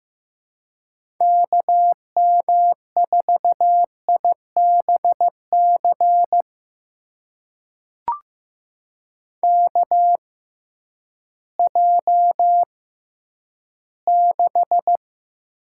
Morse code, heard as a keyed tone, about 15 wpm, spaced as for 10 wpm, 700 Hz. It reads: KM4IBC KJ6
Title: USA Call Signs - 6 Characters - 6x Character Spacing - 15wpm